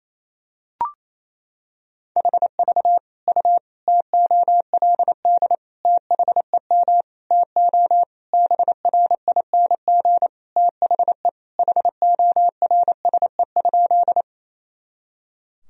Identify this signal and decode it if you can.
Morse code, heard as a keyed tone, about 27 wpm, 700 Hz. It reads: HV U TOLD THEM TO BRING THE HORSE?
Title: Sentences from Top 500 Words and Abbreviations 28wpm